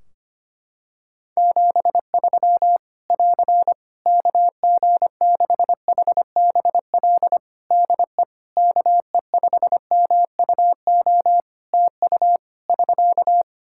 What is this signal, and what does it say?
Morse code, heard as a keyed tone, about 25 wpm, 700 Hz. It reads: 73 <AR> KG6HBL DE KE5MUO TU <SK>
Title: Practice QSO - Exchange 4 - Ending - 25wpm